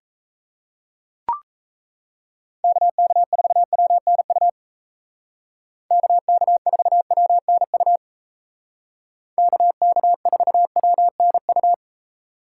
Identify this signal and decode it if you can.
Morse code, heard as a keyed tone, about 37 wpm, 700 Hz. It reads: KK4WNU KK4WNU KK4WNU